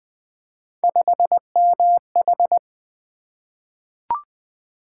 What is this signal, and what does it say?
Morse code, heard as a keyed tone, about 20 wpm, 700 Hz. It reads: 5MH
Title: Three Letter-Number Combinations - 20wpm